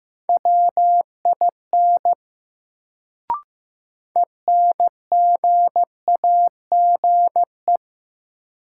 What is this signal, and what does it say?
Morse code, heard as a keyed tone, about 15 wpm, 700 Hz. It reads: WIN ENGAGE
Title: Sentences from Top 1500 Words - New Word Drill Alt - 15wpm